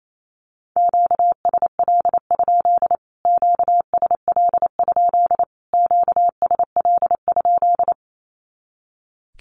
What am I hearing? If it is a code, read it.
Morse code, heard as a keyed tone, about 28 wpm, 700 Hz. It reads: QSL? QSL? QSL?
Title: Q-Codes Repeated 3x 28wpm